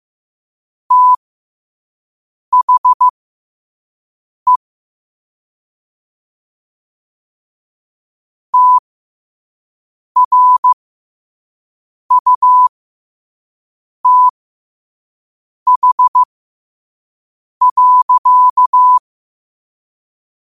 Morse code, heard as a keyed tone, about 15 wpm, spaced as for 5 wpm, 1000 Hz. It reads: THE TRUTH.
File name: qotdfiles / 5wpm